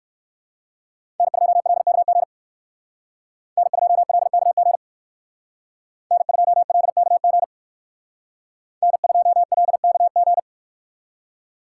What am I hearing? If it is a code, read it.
Morse code, heard as a keyed tone, about 58 wpm, 700 Hz. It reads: N2LKG N2LKG N2LKG N2LKG